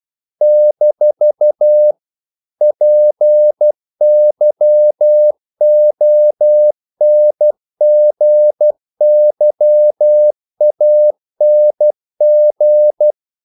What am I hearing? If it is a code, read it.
Morse code, heard as a keyed tone, about 12 wpm, 600 Hz. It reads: - PYONGYANG
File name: CW / Practice / NationalCapitals12